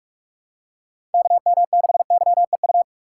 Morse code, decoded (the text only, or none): KM6YEU